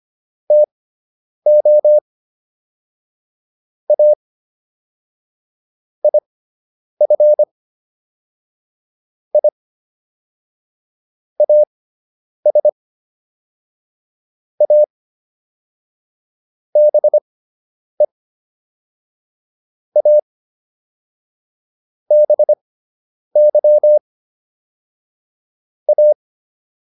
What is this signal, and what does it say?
Morse code, heard as a keyed tone, about 25 wpm, spaced as for 9 wpm, 600 Hz. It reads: TO A IF I AS A BE A BY A